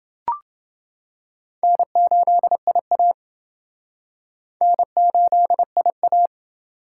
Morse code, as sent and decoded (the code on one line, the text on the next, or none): -. ---.. .. .- / -. ---.. .. .-
N8IA N8IA